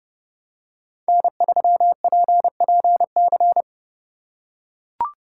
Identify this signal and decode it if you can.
Morse code, heard as a keyed tone, about 30 wpm, 700 Hz. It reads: N3PPC
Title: USA Call Signs - 4x Character Spacing - 30wpm